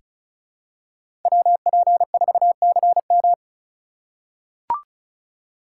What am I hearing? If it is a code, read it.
Morse code, heard as a keyed tone, about 36 wpm, 700 Hz. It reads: WP4CM
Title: USA Call Signs - 5 Characters - 3x Character Spacing - 35wpm